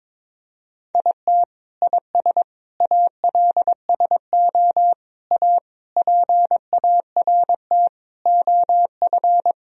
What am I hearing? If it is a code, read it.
Morse code, heard as a keyed tone, about 22 wpm, 700 Hz. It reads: IT IS ALSO A PART OF
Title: Sentences from Top 1000 Words - Review Part 1 - 22wpm